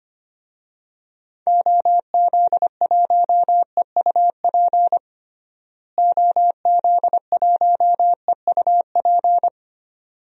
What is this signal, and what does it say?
Morse code, heard as a keyed tone, about 25 wpm, 700 Hz. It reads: OZ1EUP OZ1EUP
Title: DXCC-100 Call Signs - 6 Characters - Repeated 2x - 25wpm